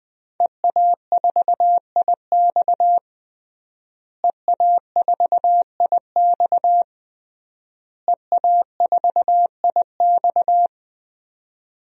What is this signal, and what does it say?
Morse code, heard as a keyed tone, about 19 wpm, 700 Hz. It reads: EA4IX EA4IX EA4IX
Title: DXCC-100 Call Signs - 5 Characters - Repeated 3x - 20wpm